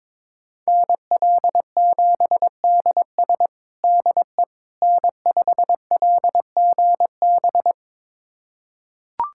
Morse code, heard as a keyed tone, about 22 wpm, 700 Hz. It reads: NL7DS DE N5LGB
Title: USA Callsign DE Callsign - 5 Characters - 22wpm